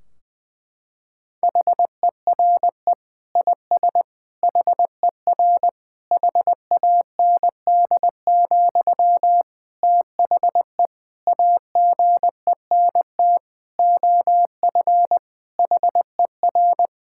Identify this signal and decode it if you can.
Morse code, heard as a keyed tone, about 20 wpm, 700 Hz. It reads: HERE IS HER HAND, THE AGENT OF HER
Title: Sentences from Top 1600 Words 20wpm